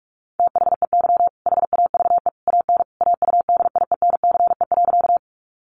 Morse code, heard as a keyed tone, about 44 wpm, 700 Hz. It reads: THEY HAVE AN AUDIENCE.
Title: Sentences from Top 800 Words 45wpm